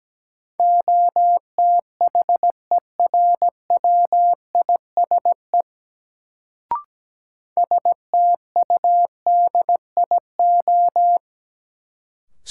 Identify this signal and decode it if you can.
Morse code, heard as a keyed tone, about 17 wpm, 700 Hz. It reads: OTHERWISE STUDIO
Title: Sentences from Top 1200 Words - New Word Drill Alt - 17wpm